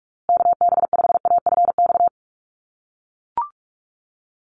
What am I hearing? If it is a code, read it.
Morse code, heard as a keyed tone, about 44 wpm, 700 Hz. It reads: KB5AFX
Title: USA Call Signs Repeated 3x 45wpm